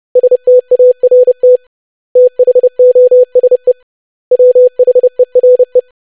Morse code, heard as a keyed tone, about 30 wpm, 500 Hz. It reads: START THOSE WHERE
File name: lesson / web / basic500-25wpm